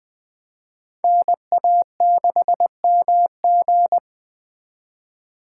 Morse code, heard as a keyed tone, about 20 wpm, 700 Hz. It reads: NA6MG
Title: 2023 SOTA Activator Call Signs - 20wpm